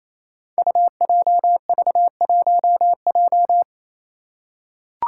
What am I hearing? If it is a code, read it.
Morse code, heard as a keyed tone, about 28 wpm, 700 Hz. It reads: UJV1J
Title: Five Challenging Letter-Number Combinations 28wpm